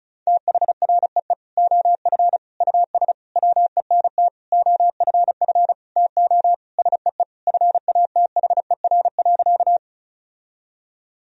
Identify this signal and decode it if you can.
Morse code, heard as a keyed tone, about 36 wpm, 700 Hz. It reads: THREE OF US WENT OFF TO SEE FATHER.
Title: Sentences from Top 1000 Words - Review Part 1 - 35wpm